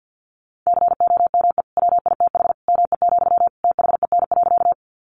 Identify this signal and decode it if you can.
Morse code, heard as a keyed tone, about 50 wpm, 700 Hz. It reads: COME WITH ME, THEN.
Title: Sentences from Top 200 Words 50wpm